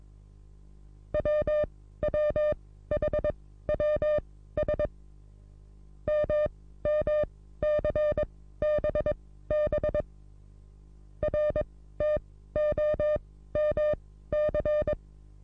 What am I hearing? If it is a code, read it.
Morse code, heard as a keyed tone, about 22 wpm, spaced as for 14 wpm, 600 Hz. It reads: WWHWS MMCBB RTOMC
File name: repos / Morsekurs / DJ4UF Morsen27